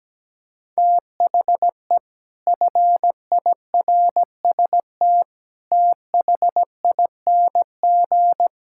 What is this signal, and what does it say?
Morse code, heard as a keyed tone, about 17 wpm, 700 Hz. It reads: THE FIRST THING
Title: Sentences from Top 1000 Words - Review Part 6 - 17wpm